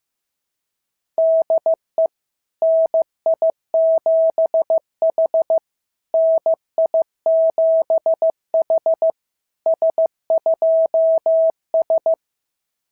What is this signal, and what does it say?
Morse code, heard as a keyed tone, about 14 wpm, 650 Hz. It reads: DE NI7H NI7H S2S
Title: SOTA QSO Practice 15wpm